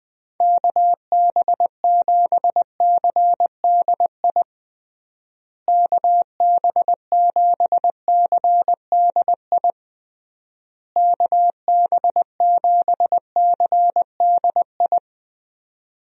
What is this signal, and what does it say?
Morse code, heard as a keyed tone, about 21 wpm, 700 Hz. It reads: KB7CDI KB7CDI KB7CDI
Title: USA Call Signs - 6 Characters - Repeated 3x - 20wpm